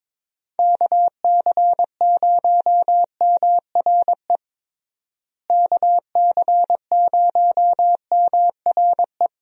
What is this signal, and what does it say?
Morse code, heard as a keyed tone, about 22 wpm, 700 Hz. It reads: KC0MRE KC0MRE
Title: USA Call Signs - 6 Characters - Repeated 2x - 22wpm